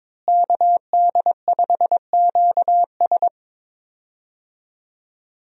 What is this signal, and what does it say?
Morse code, heard as a keyed tone, about 22 wpm, 700 Hz. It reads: KD5QS
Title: USA Call Signs - 5 Characters - Repeated 2x - 22wpm